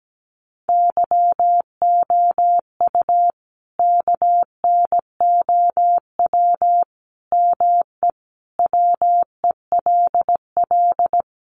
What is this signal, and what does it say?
Morse code, heard as a keyed tone, about 17 wpm, 700 Hz. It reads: YOU KNOW ME WELL
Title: Sentences from Top 400 Words 17wpm